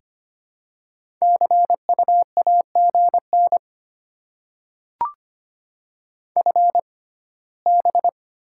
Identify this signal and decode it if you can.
Morse code, heard as a keyed tone, about 25 wpm, spaced as for 17 wpm, 700 Hz. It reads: CU AGN FB
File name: Top 20 QSO Elements - 6x Character Spacing - 25wpm